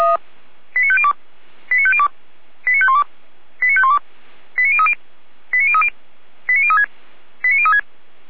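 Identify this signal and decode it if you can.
Five-tone selective call (ZVEI), heard as quick runs of five tones, tones of about 70 ms. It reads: 89502 89502 89512 89512 89030 89030 89036 89036
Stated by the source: ZVEI five-tone selective call received with an SDR.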